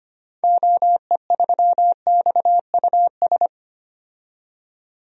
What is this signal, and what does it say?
Morse code, heard as a keyed tone, about 25 wpm, 690 Hz. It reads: OE3XUS
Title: DXCC-100 Call Signs - 6 Characters - 25wpm